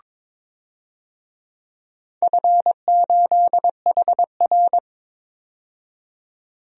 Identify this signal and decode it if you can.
Morse code, heard as a keyed tone, about 21 wpm, 690 Hz. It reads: F8HR
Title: DXCC-100 Call Signs - 4 Characters - 22wpm